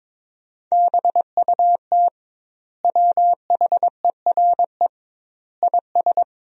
Morse code, heard as a keyed tone, about 22 wpm, 700 Hz. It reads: BUT WHERE IS
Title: 2x Word-Spacing - Sentences from Top 200 Words - 22wpm